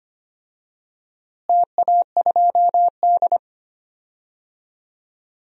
Morse code, heard as a keyed tone, about 25 wpm, 700 Hz. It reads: TA2D
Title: DXCC-100 Call Signs - 4 Characters - Repeated 3x - 25wpm